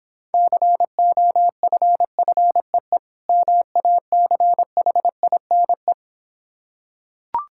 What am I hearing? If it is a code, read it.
Morse code, heard as a keyed tone, about 26 wpm, 700 Hz. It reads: COFFEE MACHINE